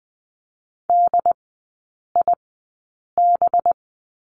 Morse code, heard as a keyed tone, about 20 wpm, spaced as for 8 wpm, 690 Hz. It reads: DIB
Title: 212 Lesson 34e - Number 8 - Three Characters 20wpm